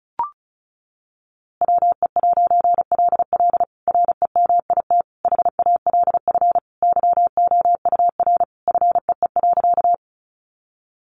Morse code, heard as a keyed tone, about 34 wpm, 700 Hz. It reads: WE'LL REMIT HALF YOUR FEE.
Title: A Sound Of Thunder 35wpm